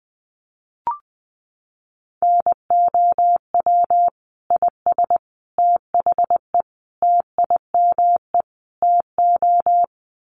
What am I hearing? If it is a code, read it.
Morse code, heard as a keyed tone, about 20 wpm, 700 Hz. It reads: NOW IS THE TIME TO
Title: Sets of 5 Words Encore2 20wpm